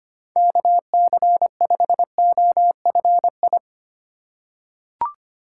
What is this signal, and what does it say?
Morse code, heard as a keyed tone, about 25 wpm, 700 Hz. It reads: KC5OFI
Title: USA Call Signs - 6 Characters - 4x Character Spacing - 25wpm